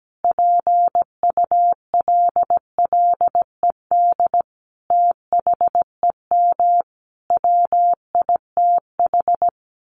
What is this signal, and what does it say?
Morse code, heard as a keyed tone, about 17 wpm, 700 Hz. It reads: PULLED THEM WITH
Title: A Sound Of Thunder 17wpm